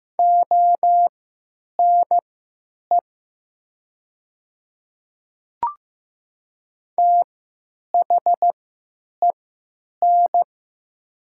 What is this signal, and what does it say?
Morse code, heard as a keyed tone, about 15 wpm, spaced as for 8 wpm, 700 Hz. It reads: ONE THEN
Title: 3x Character Spacing - Top 100 Words - 15wpm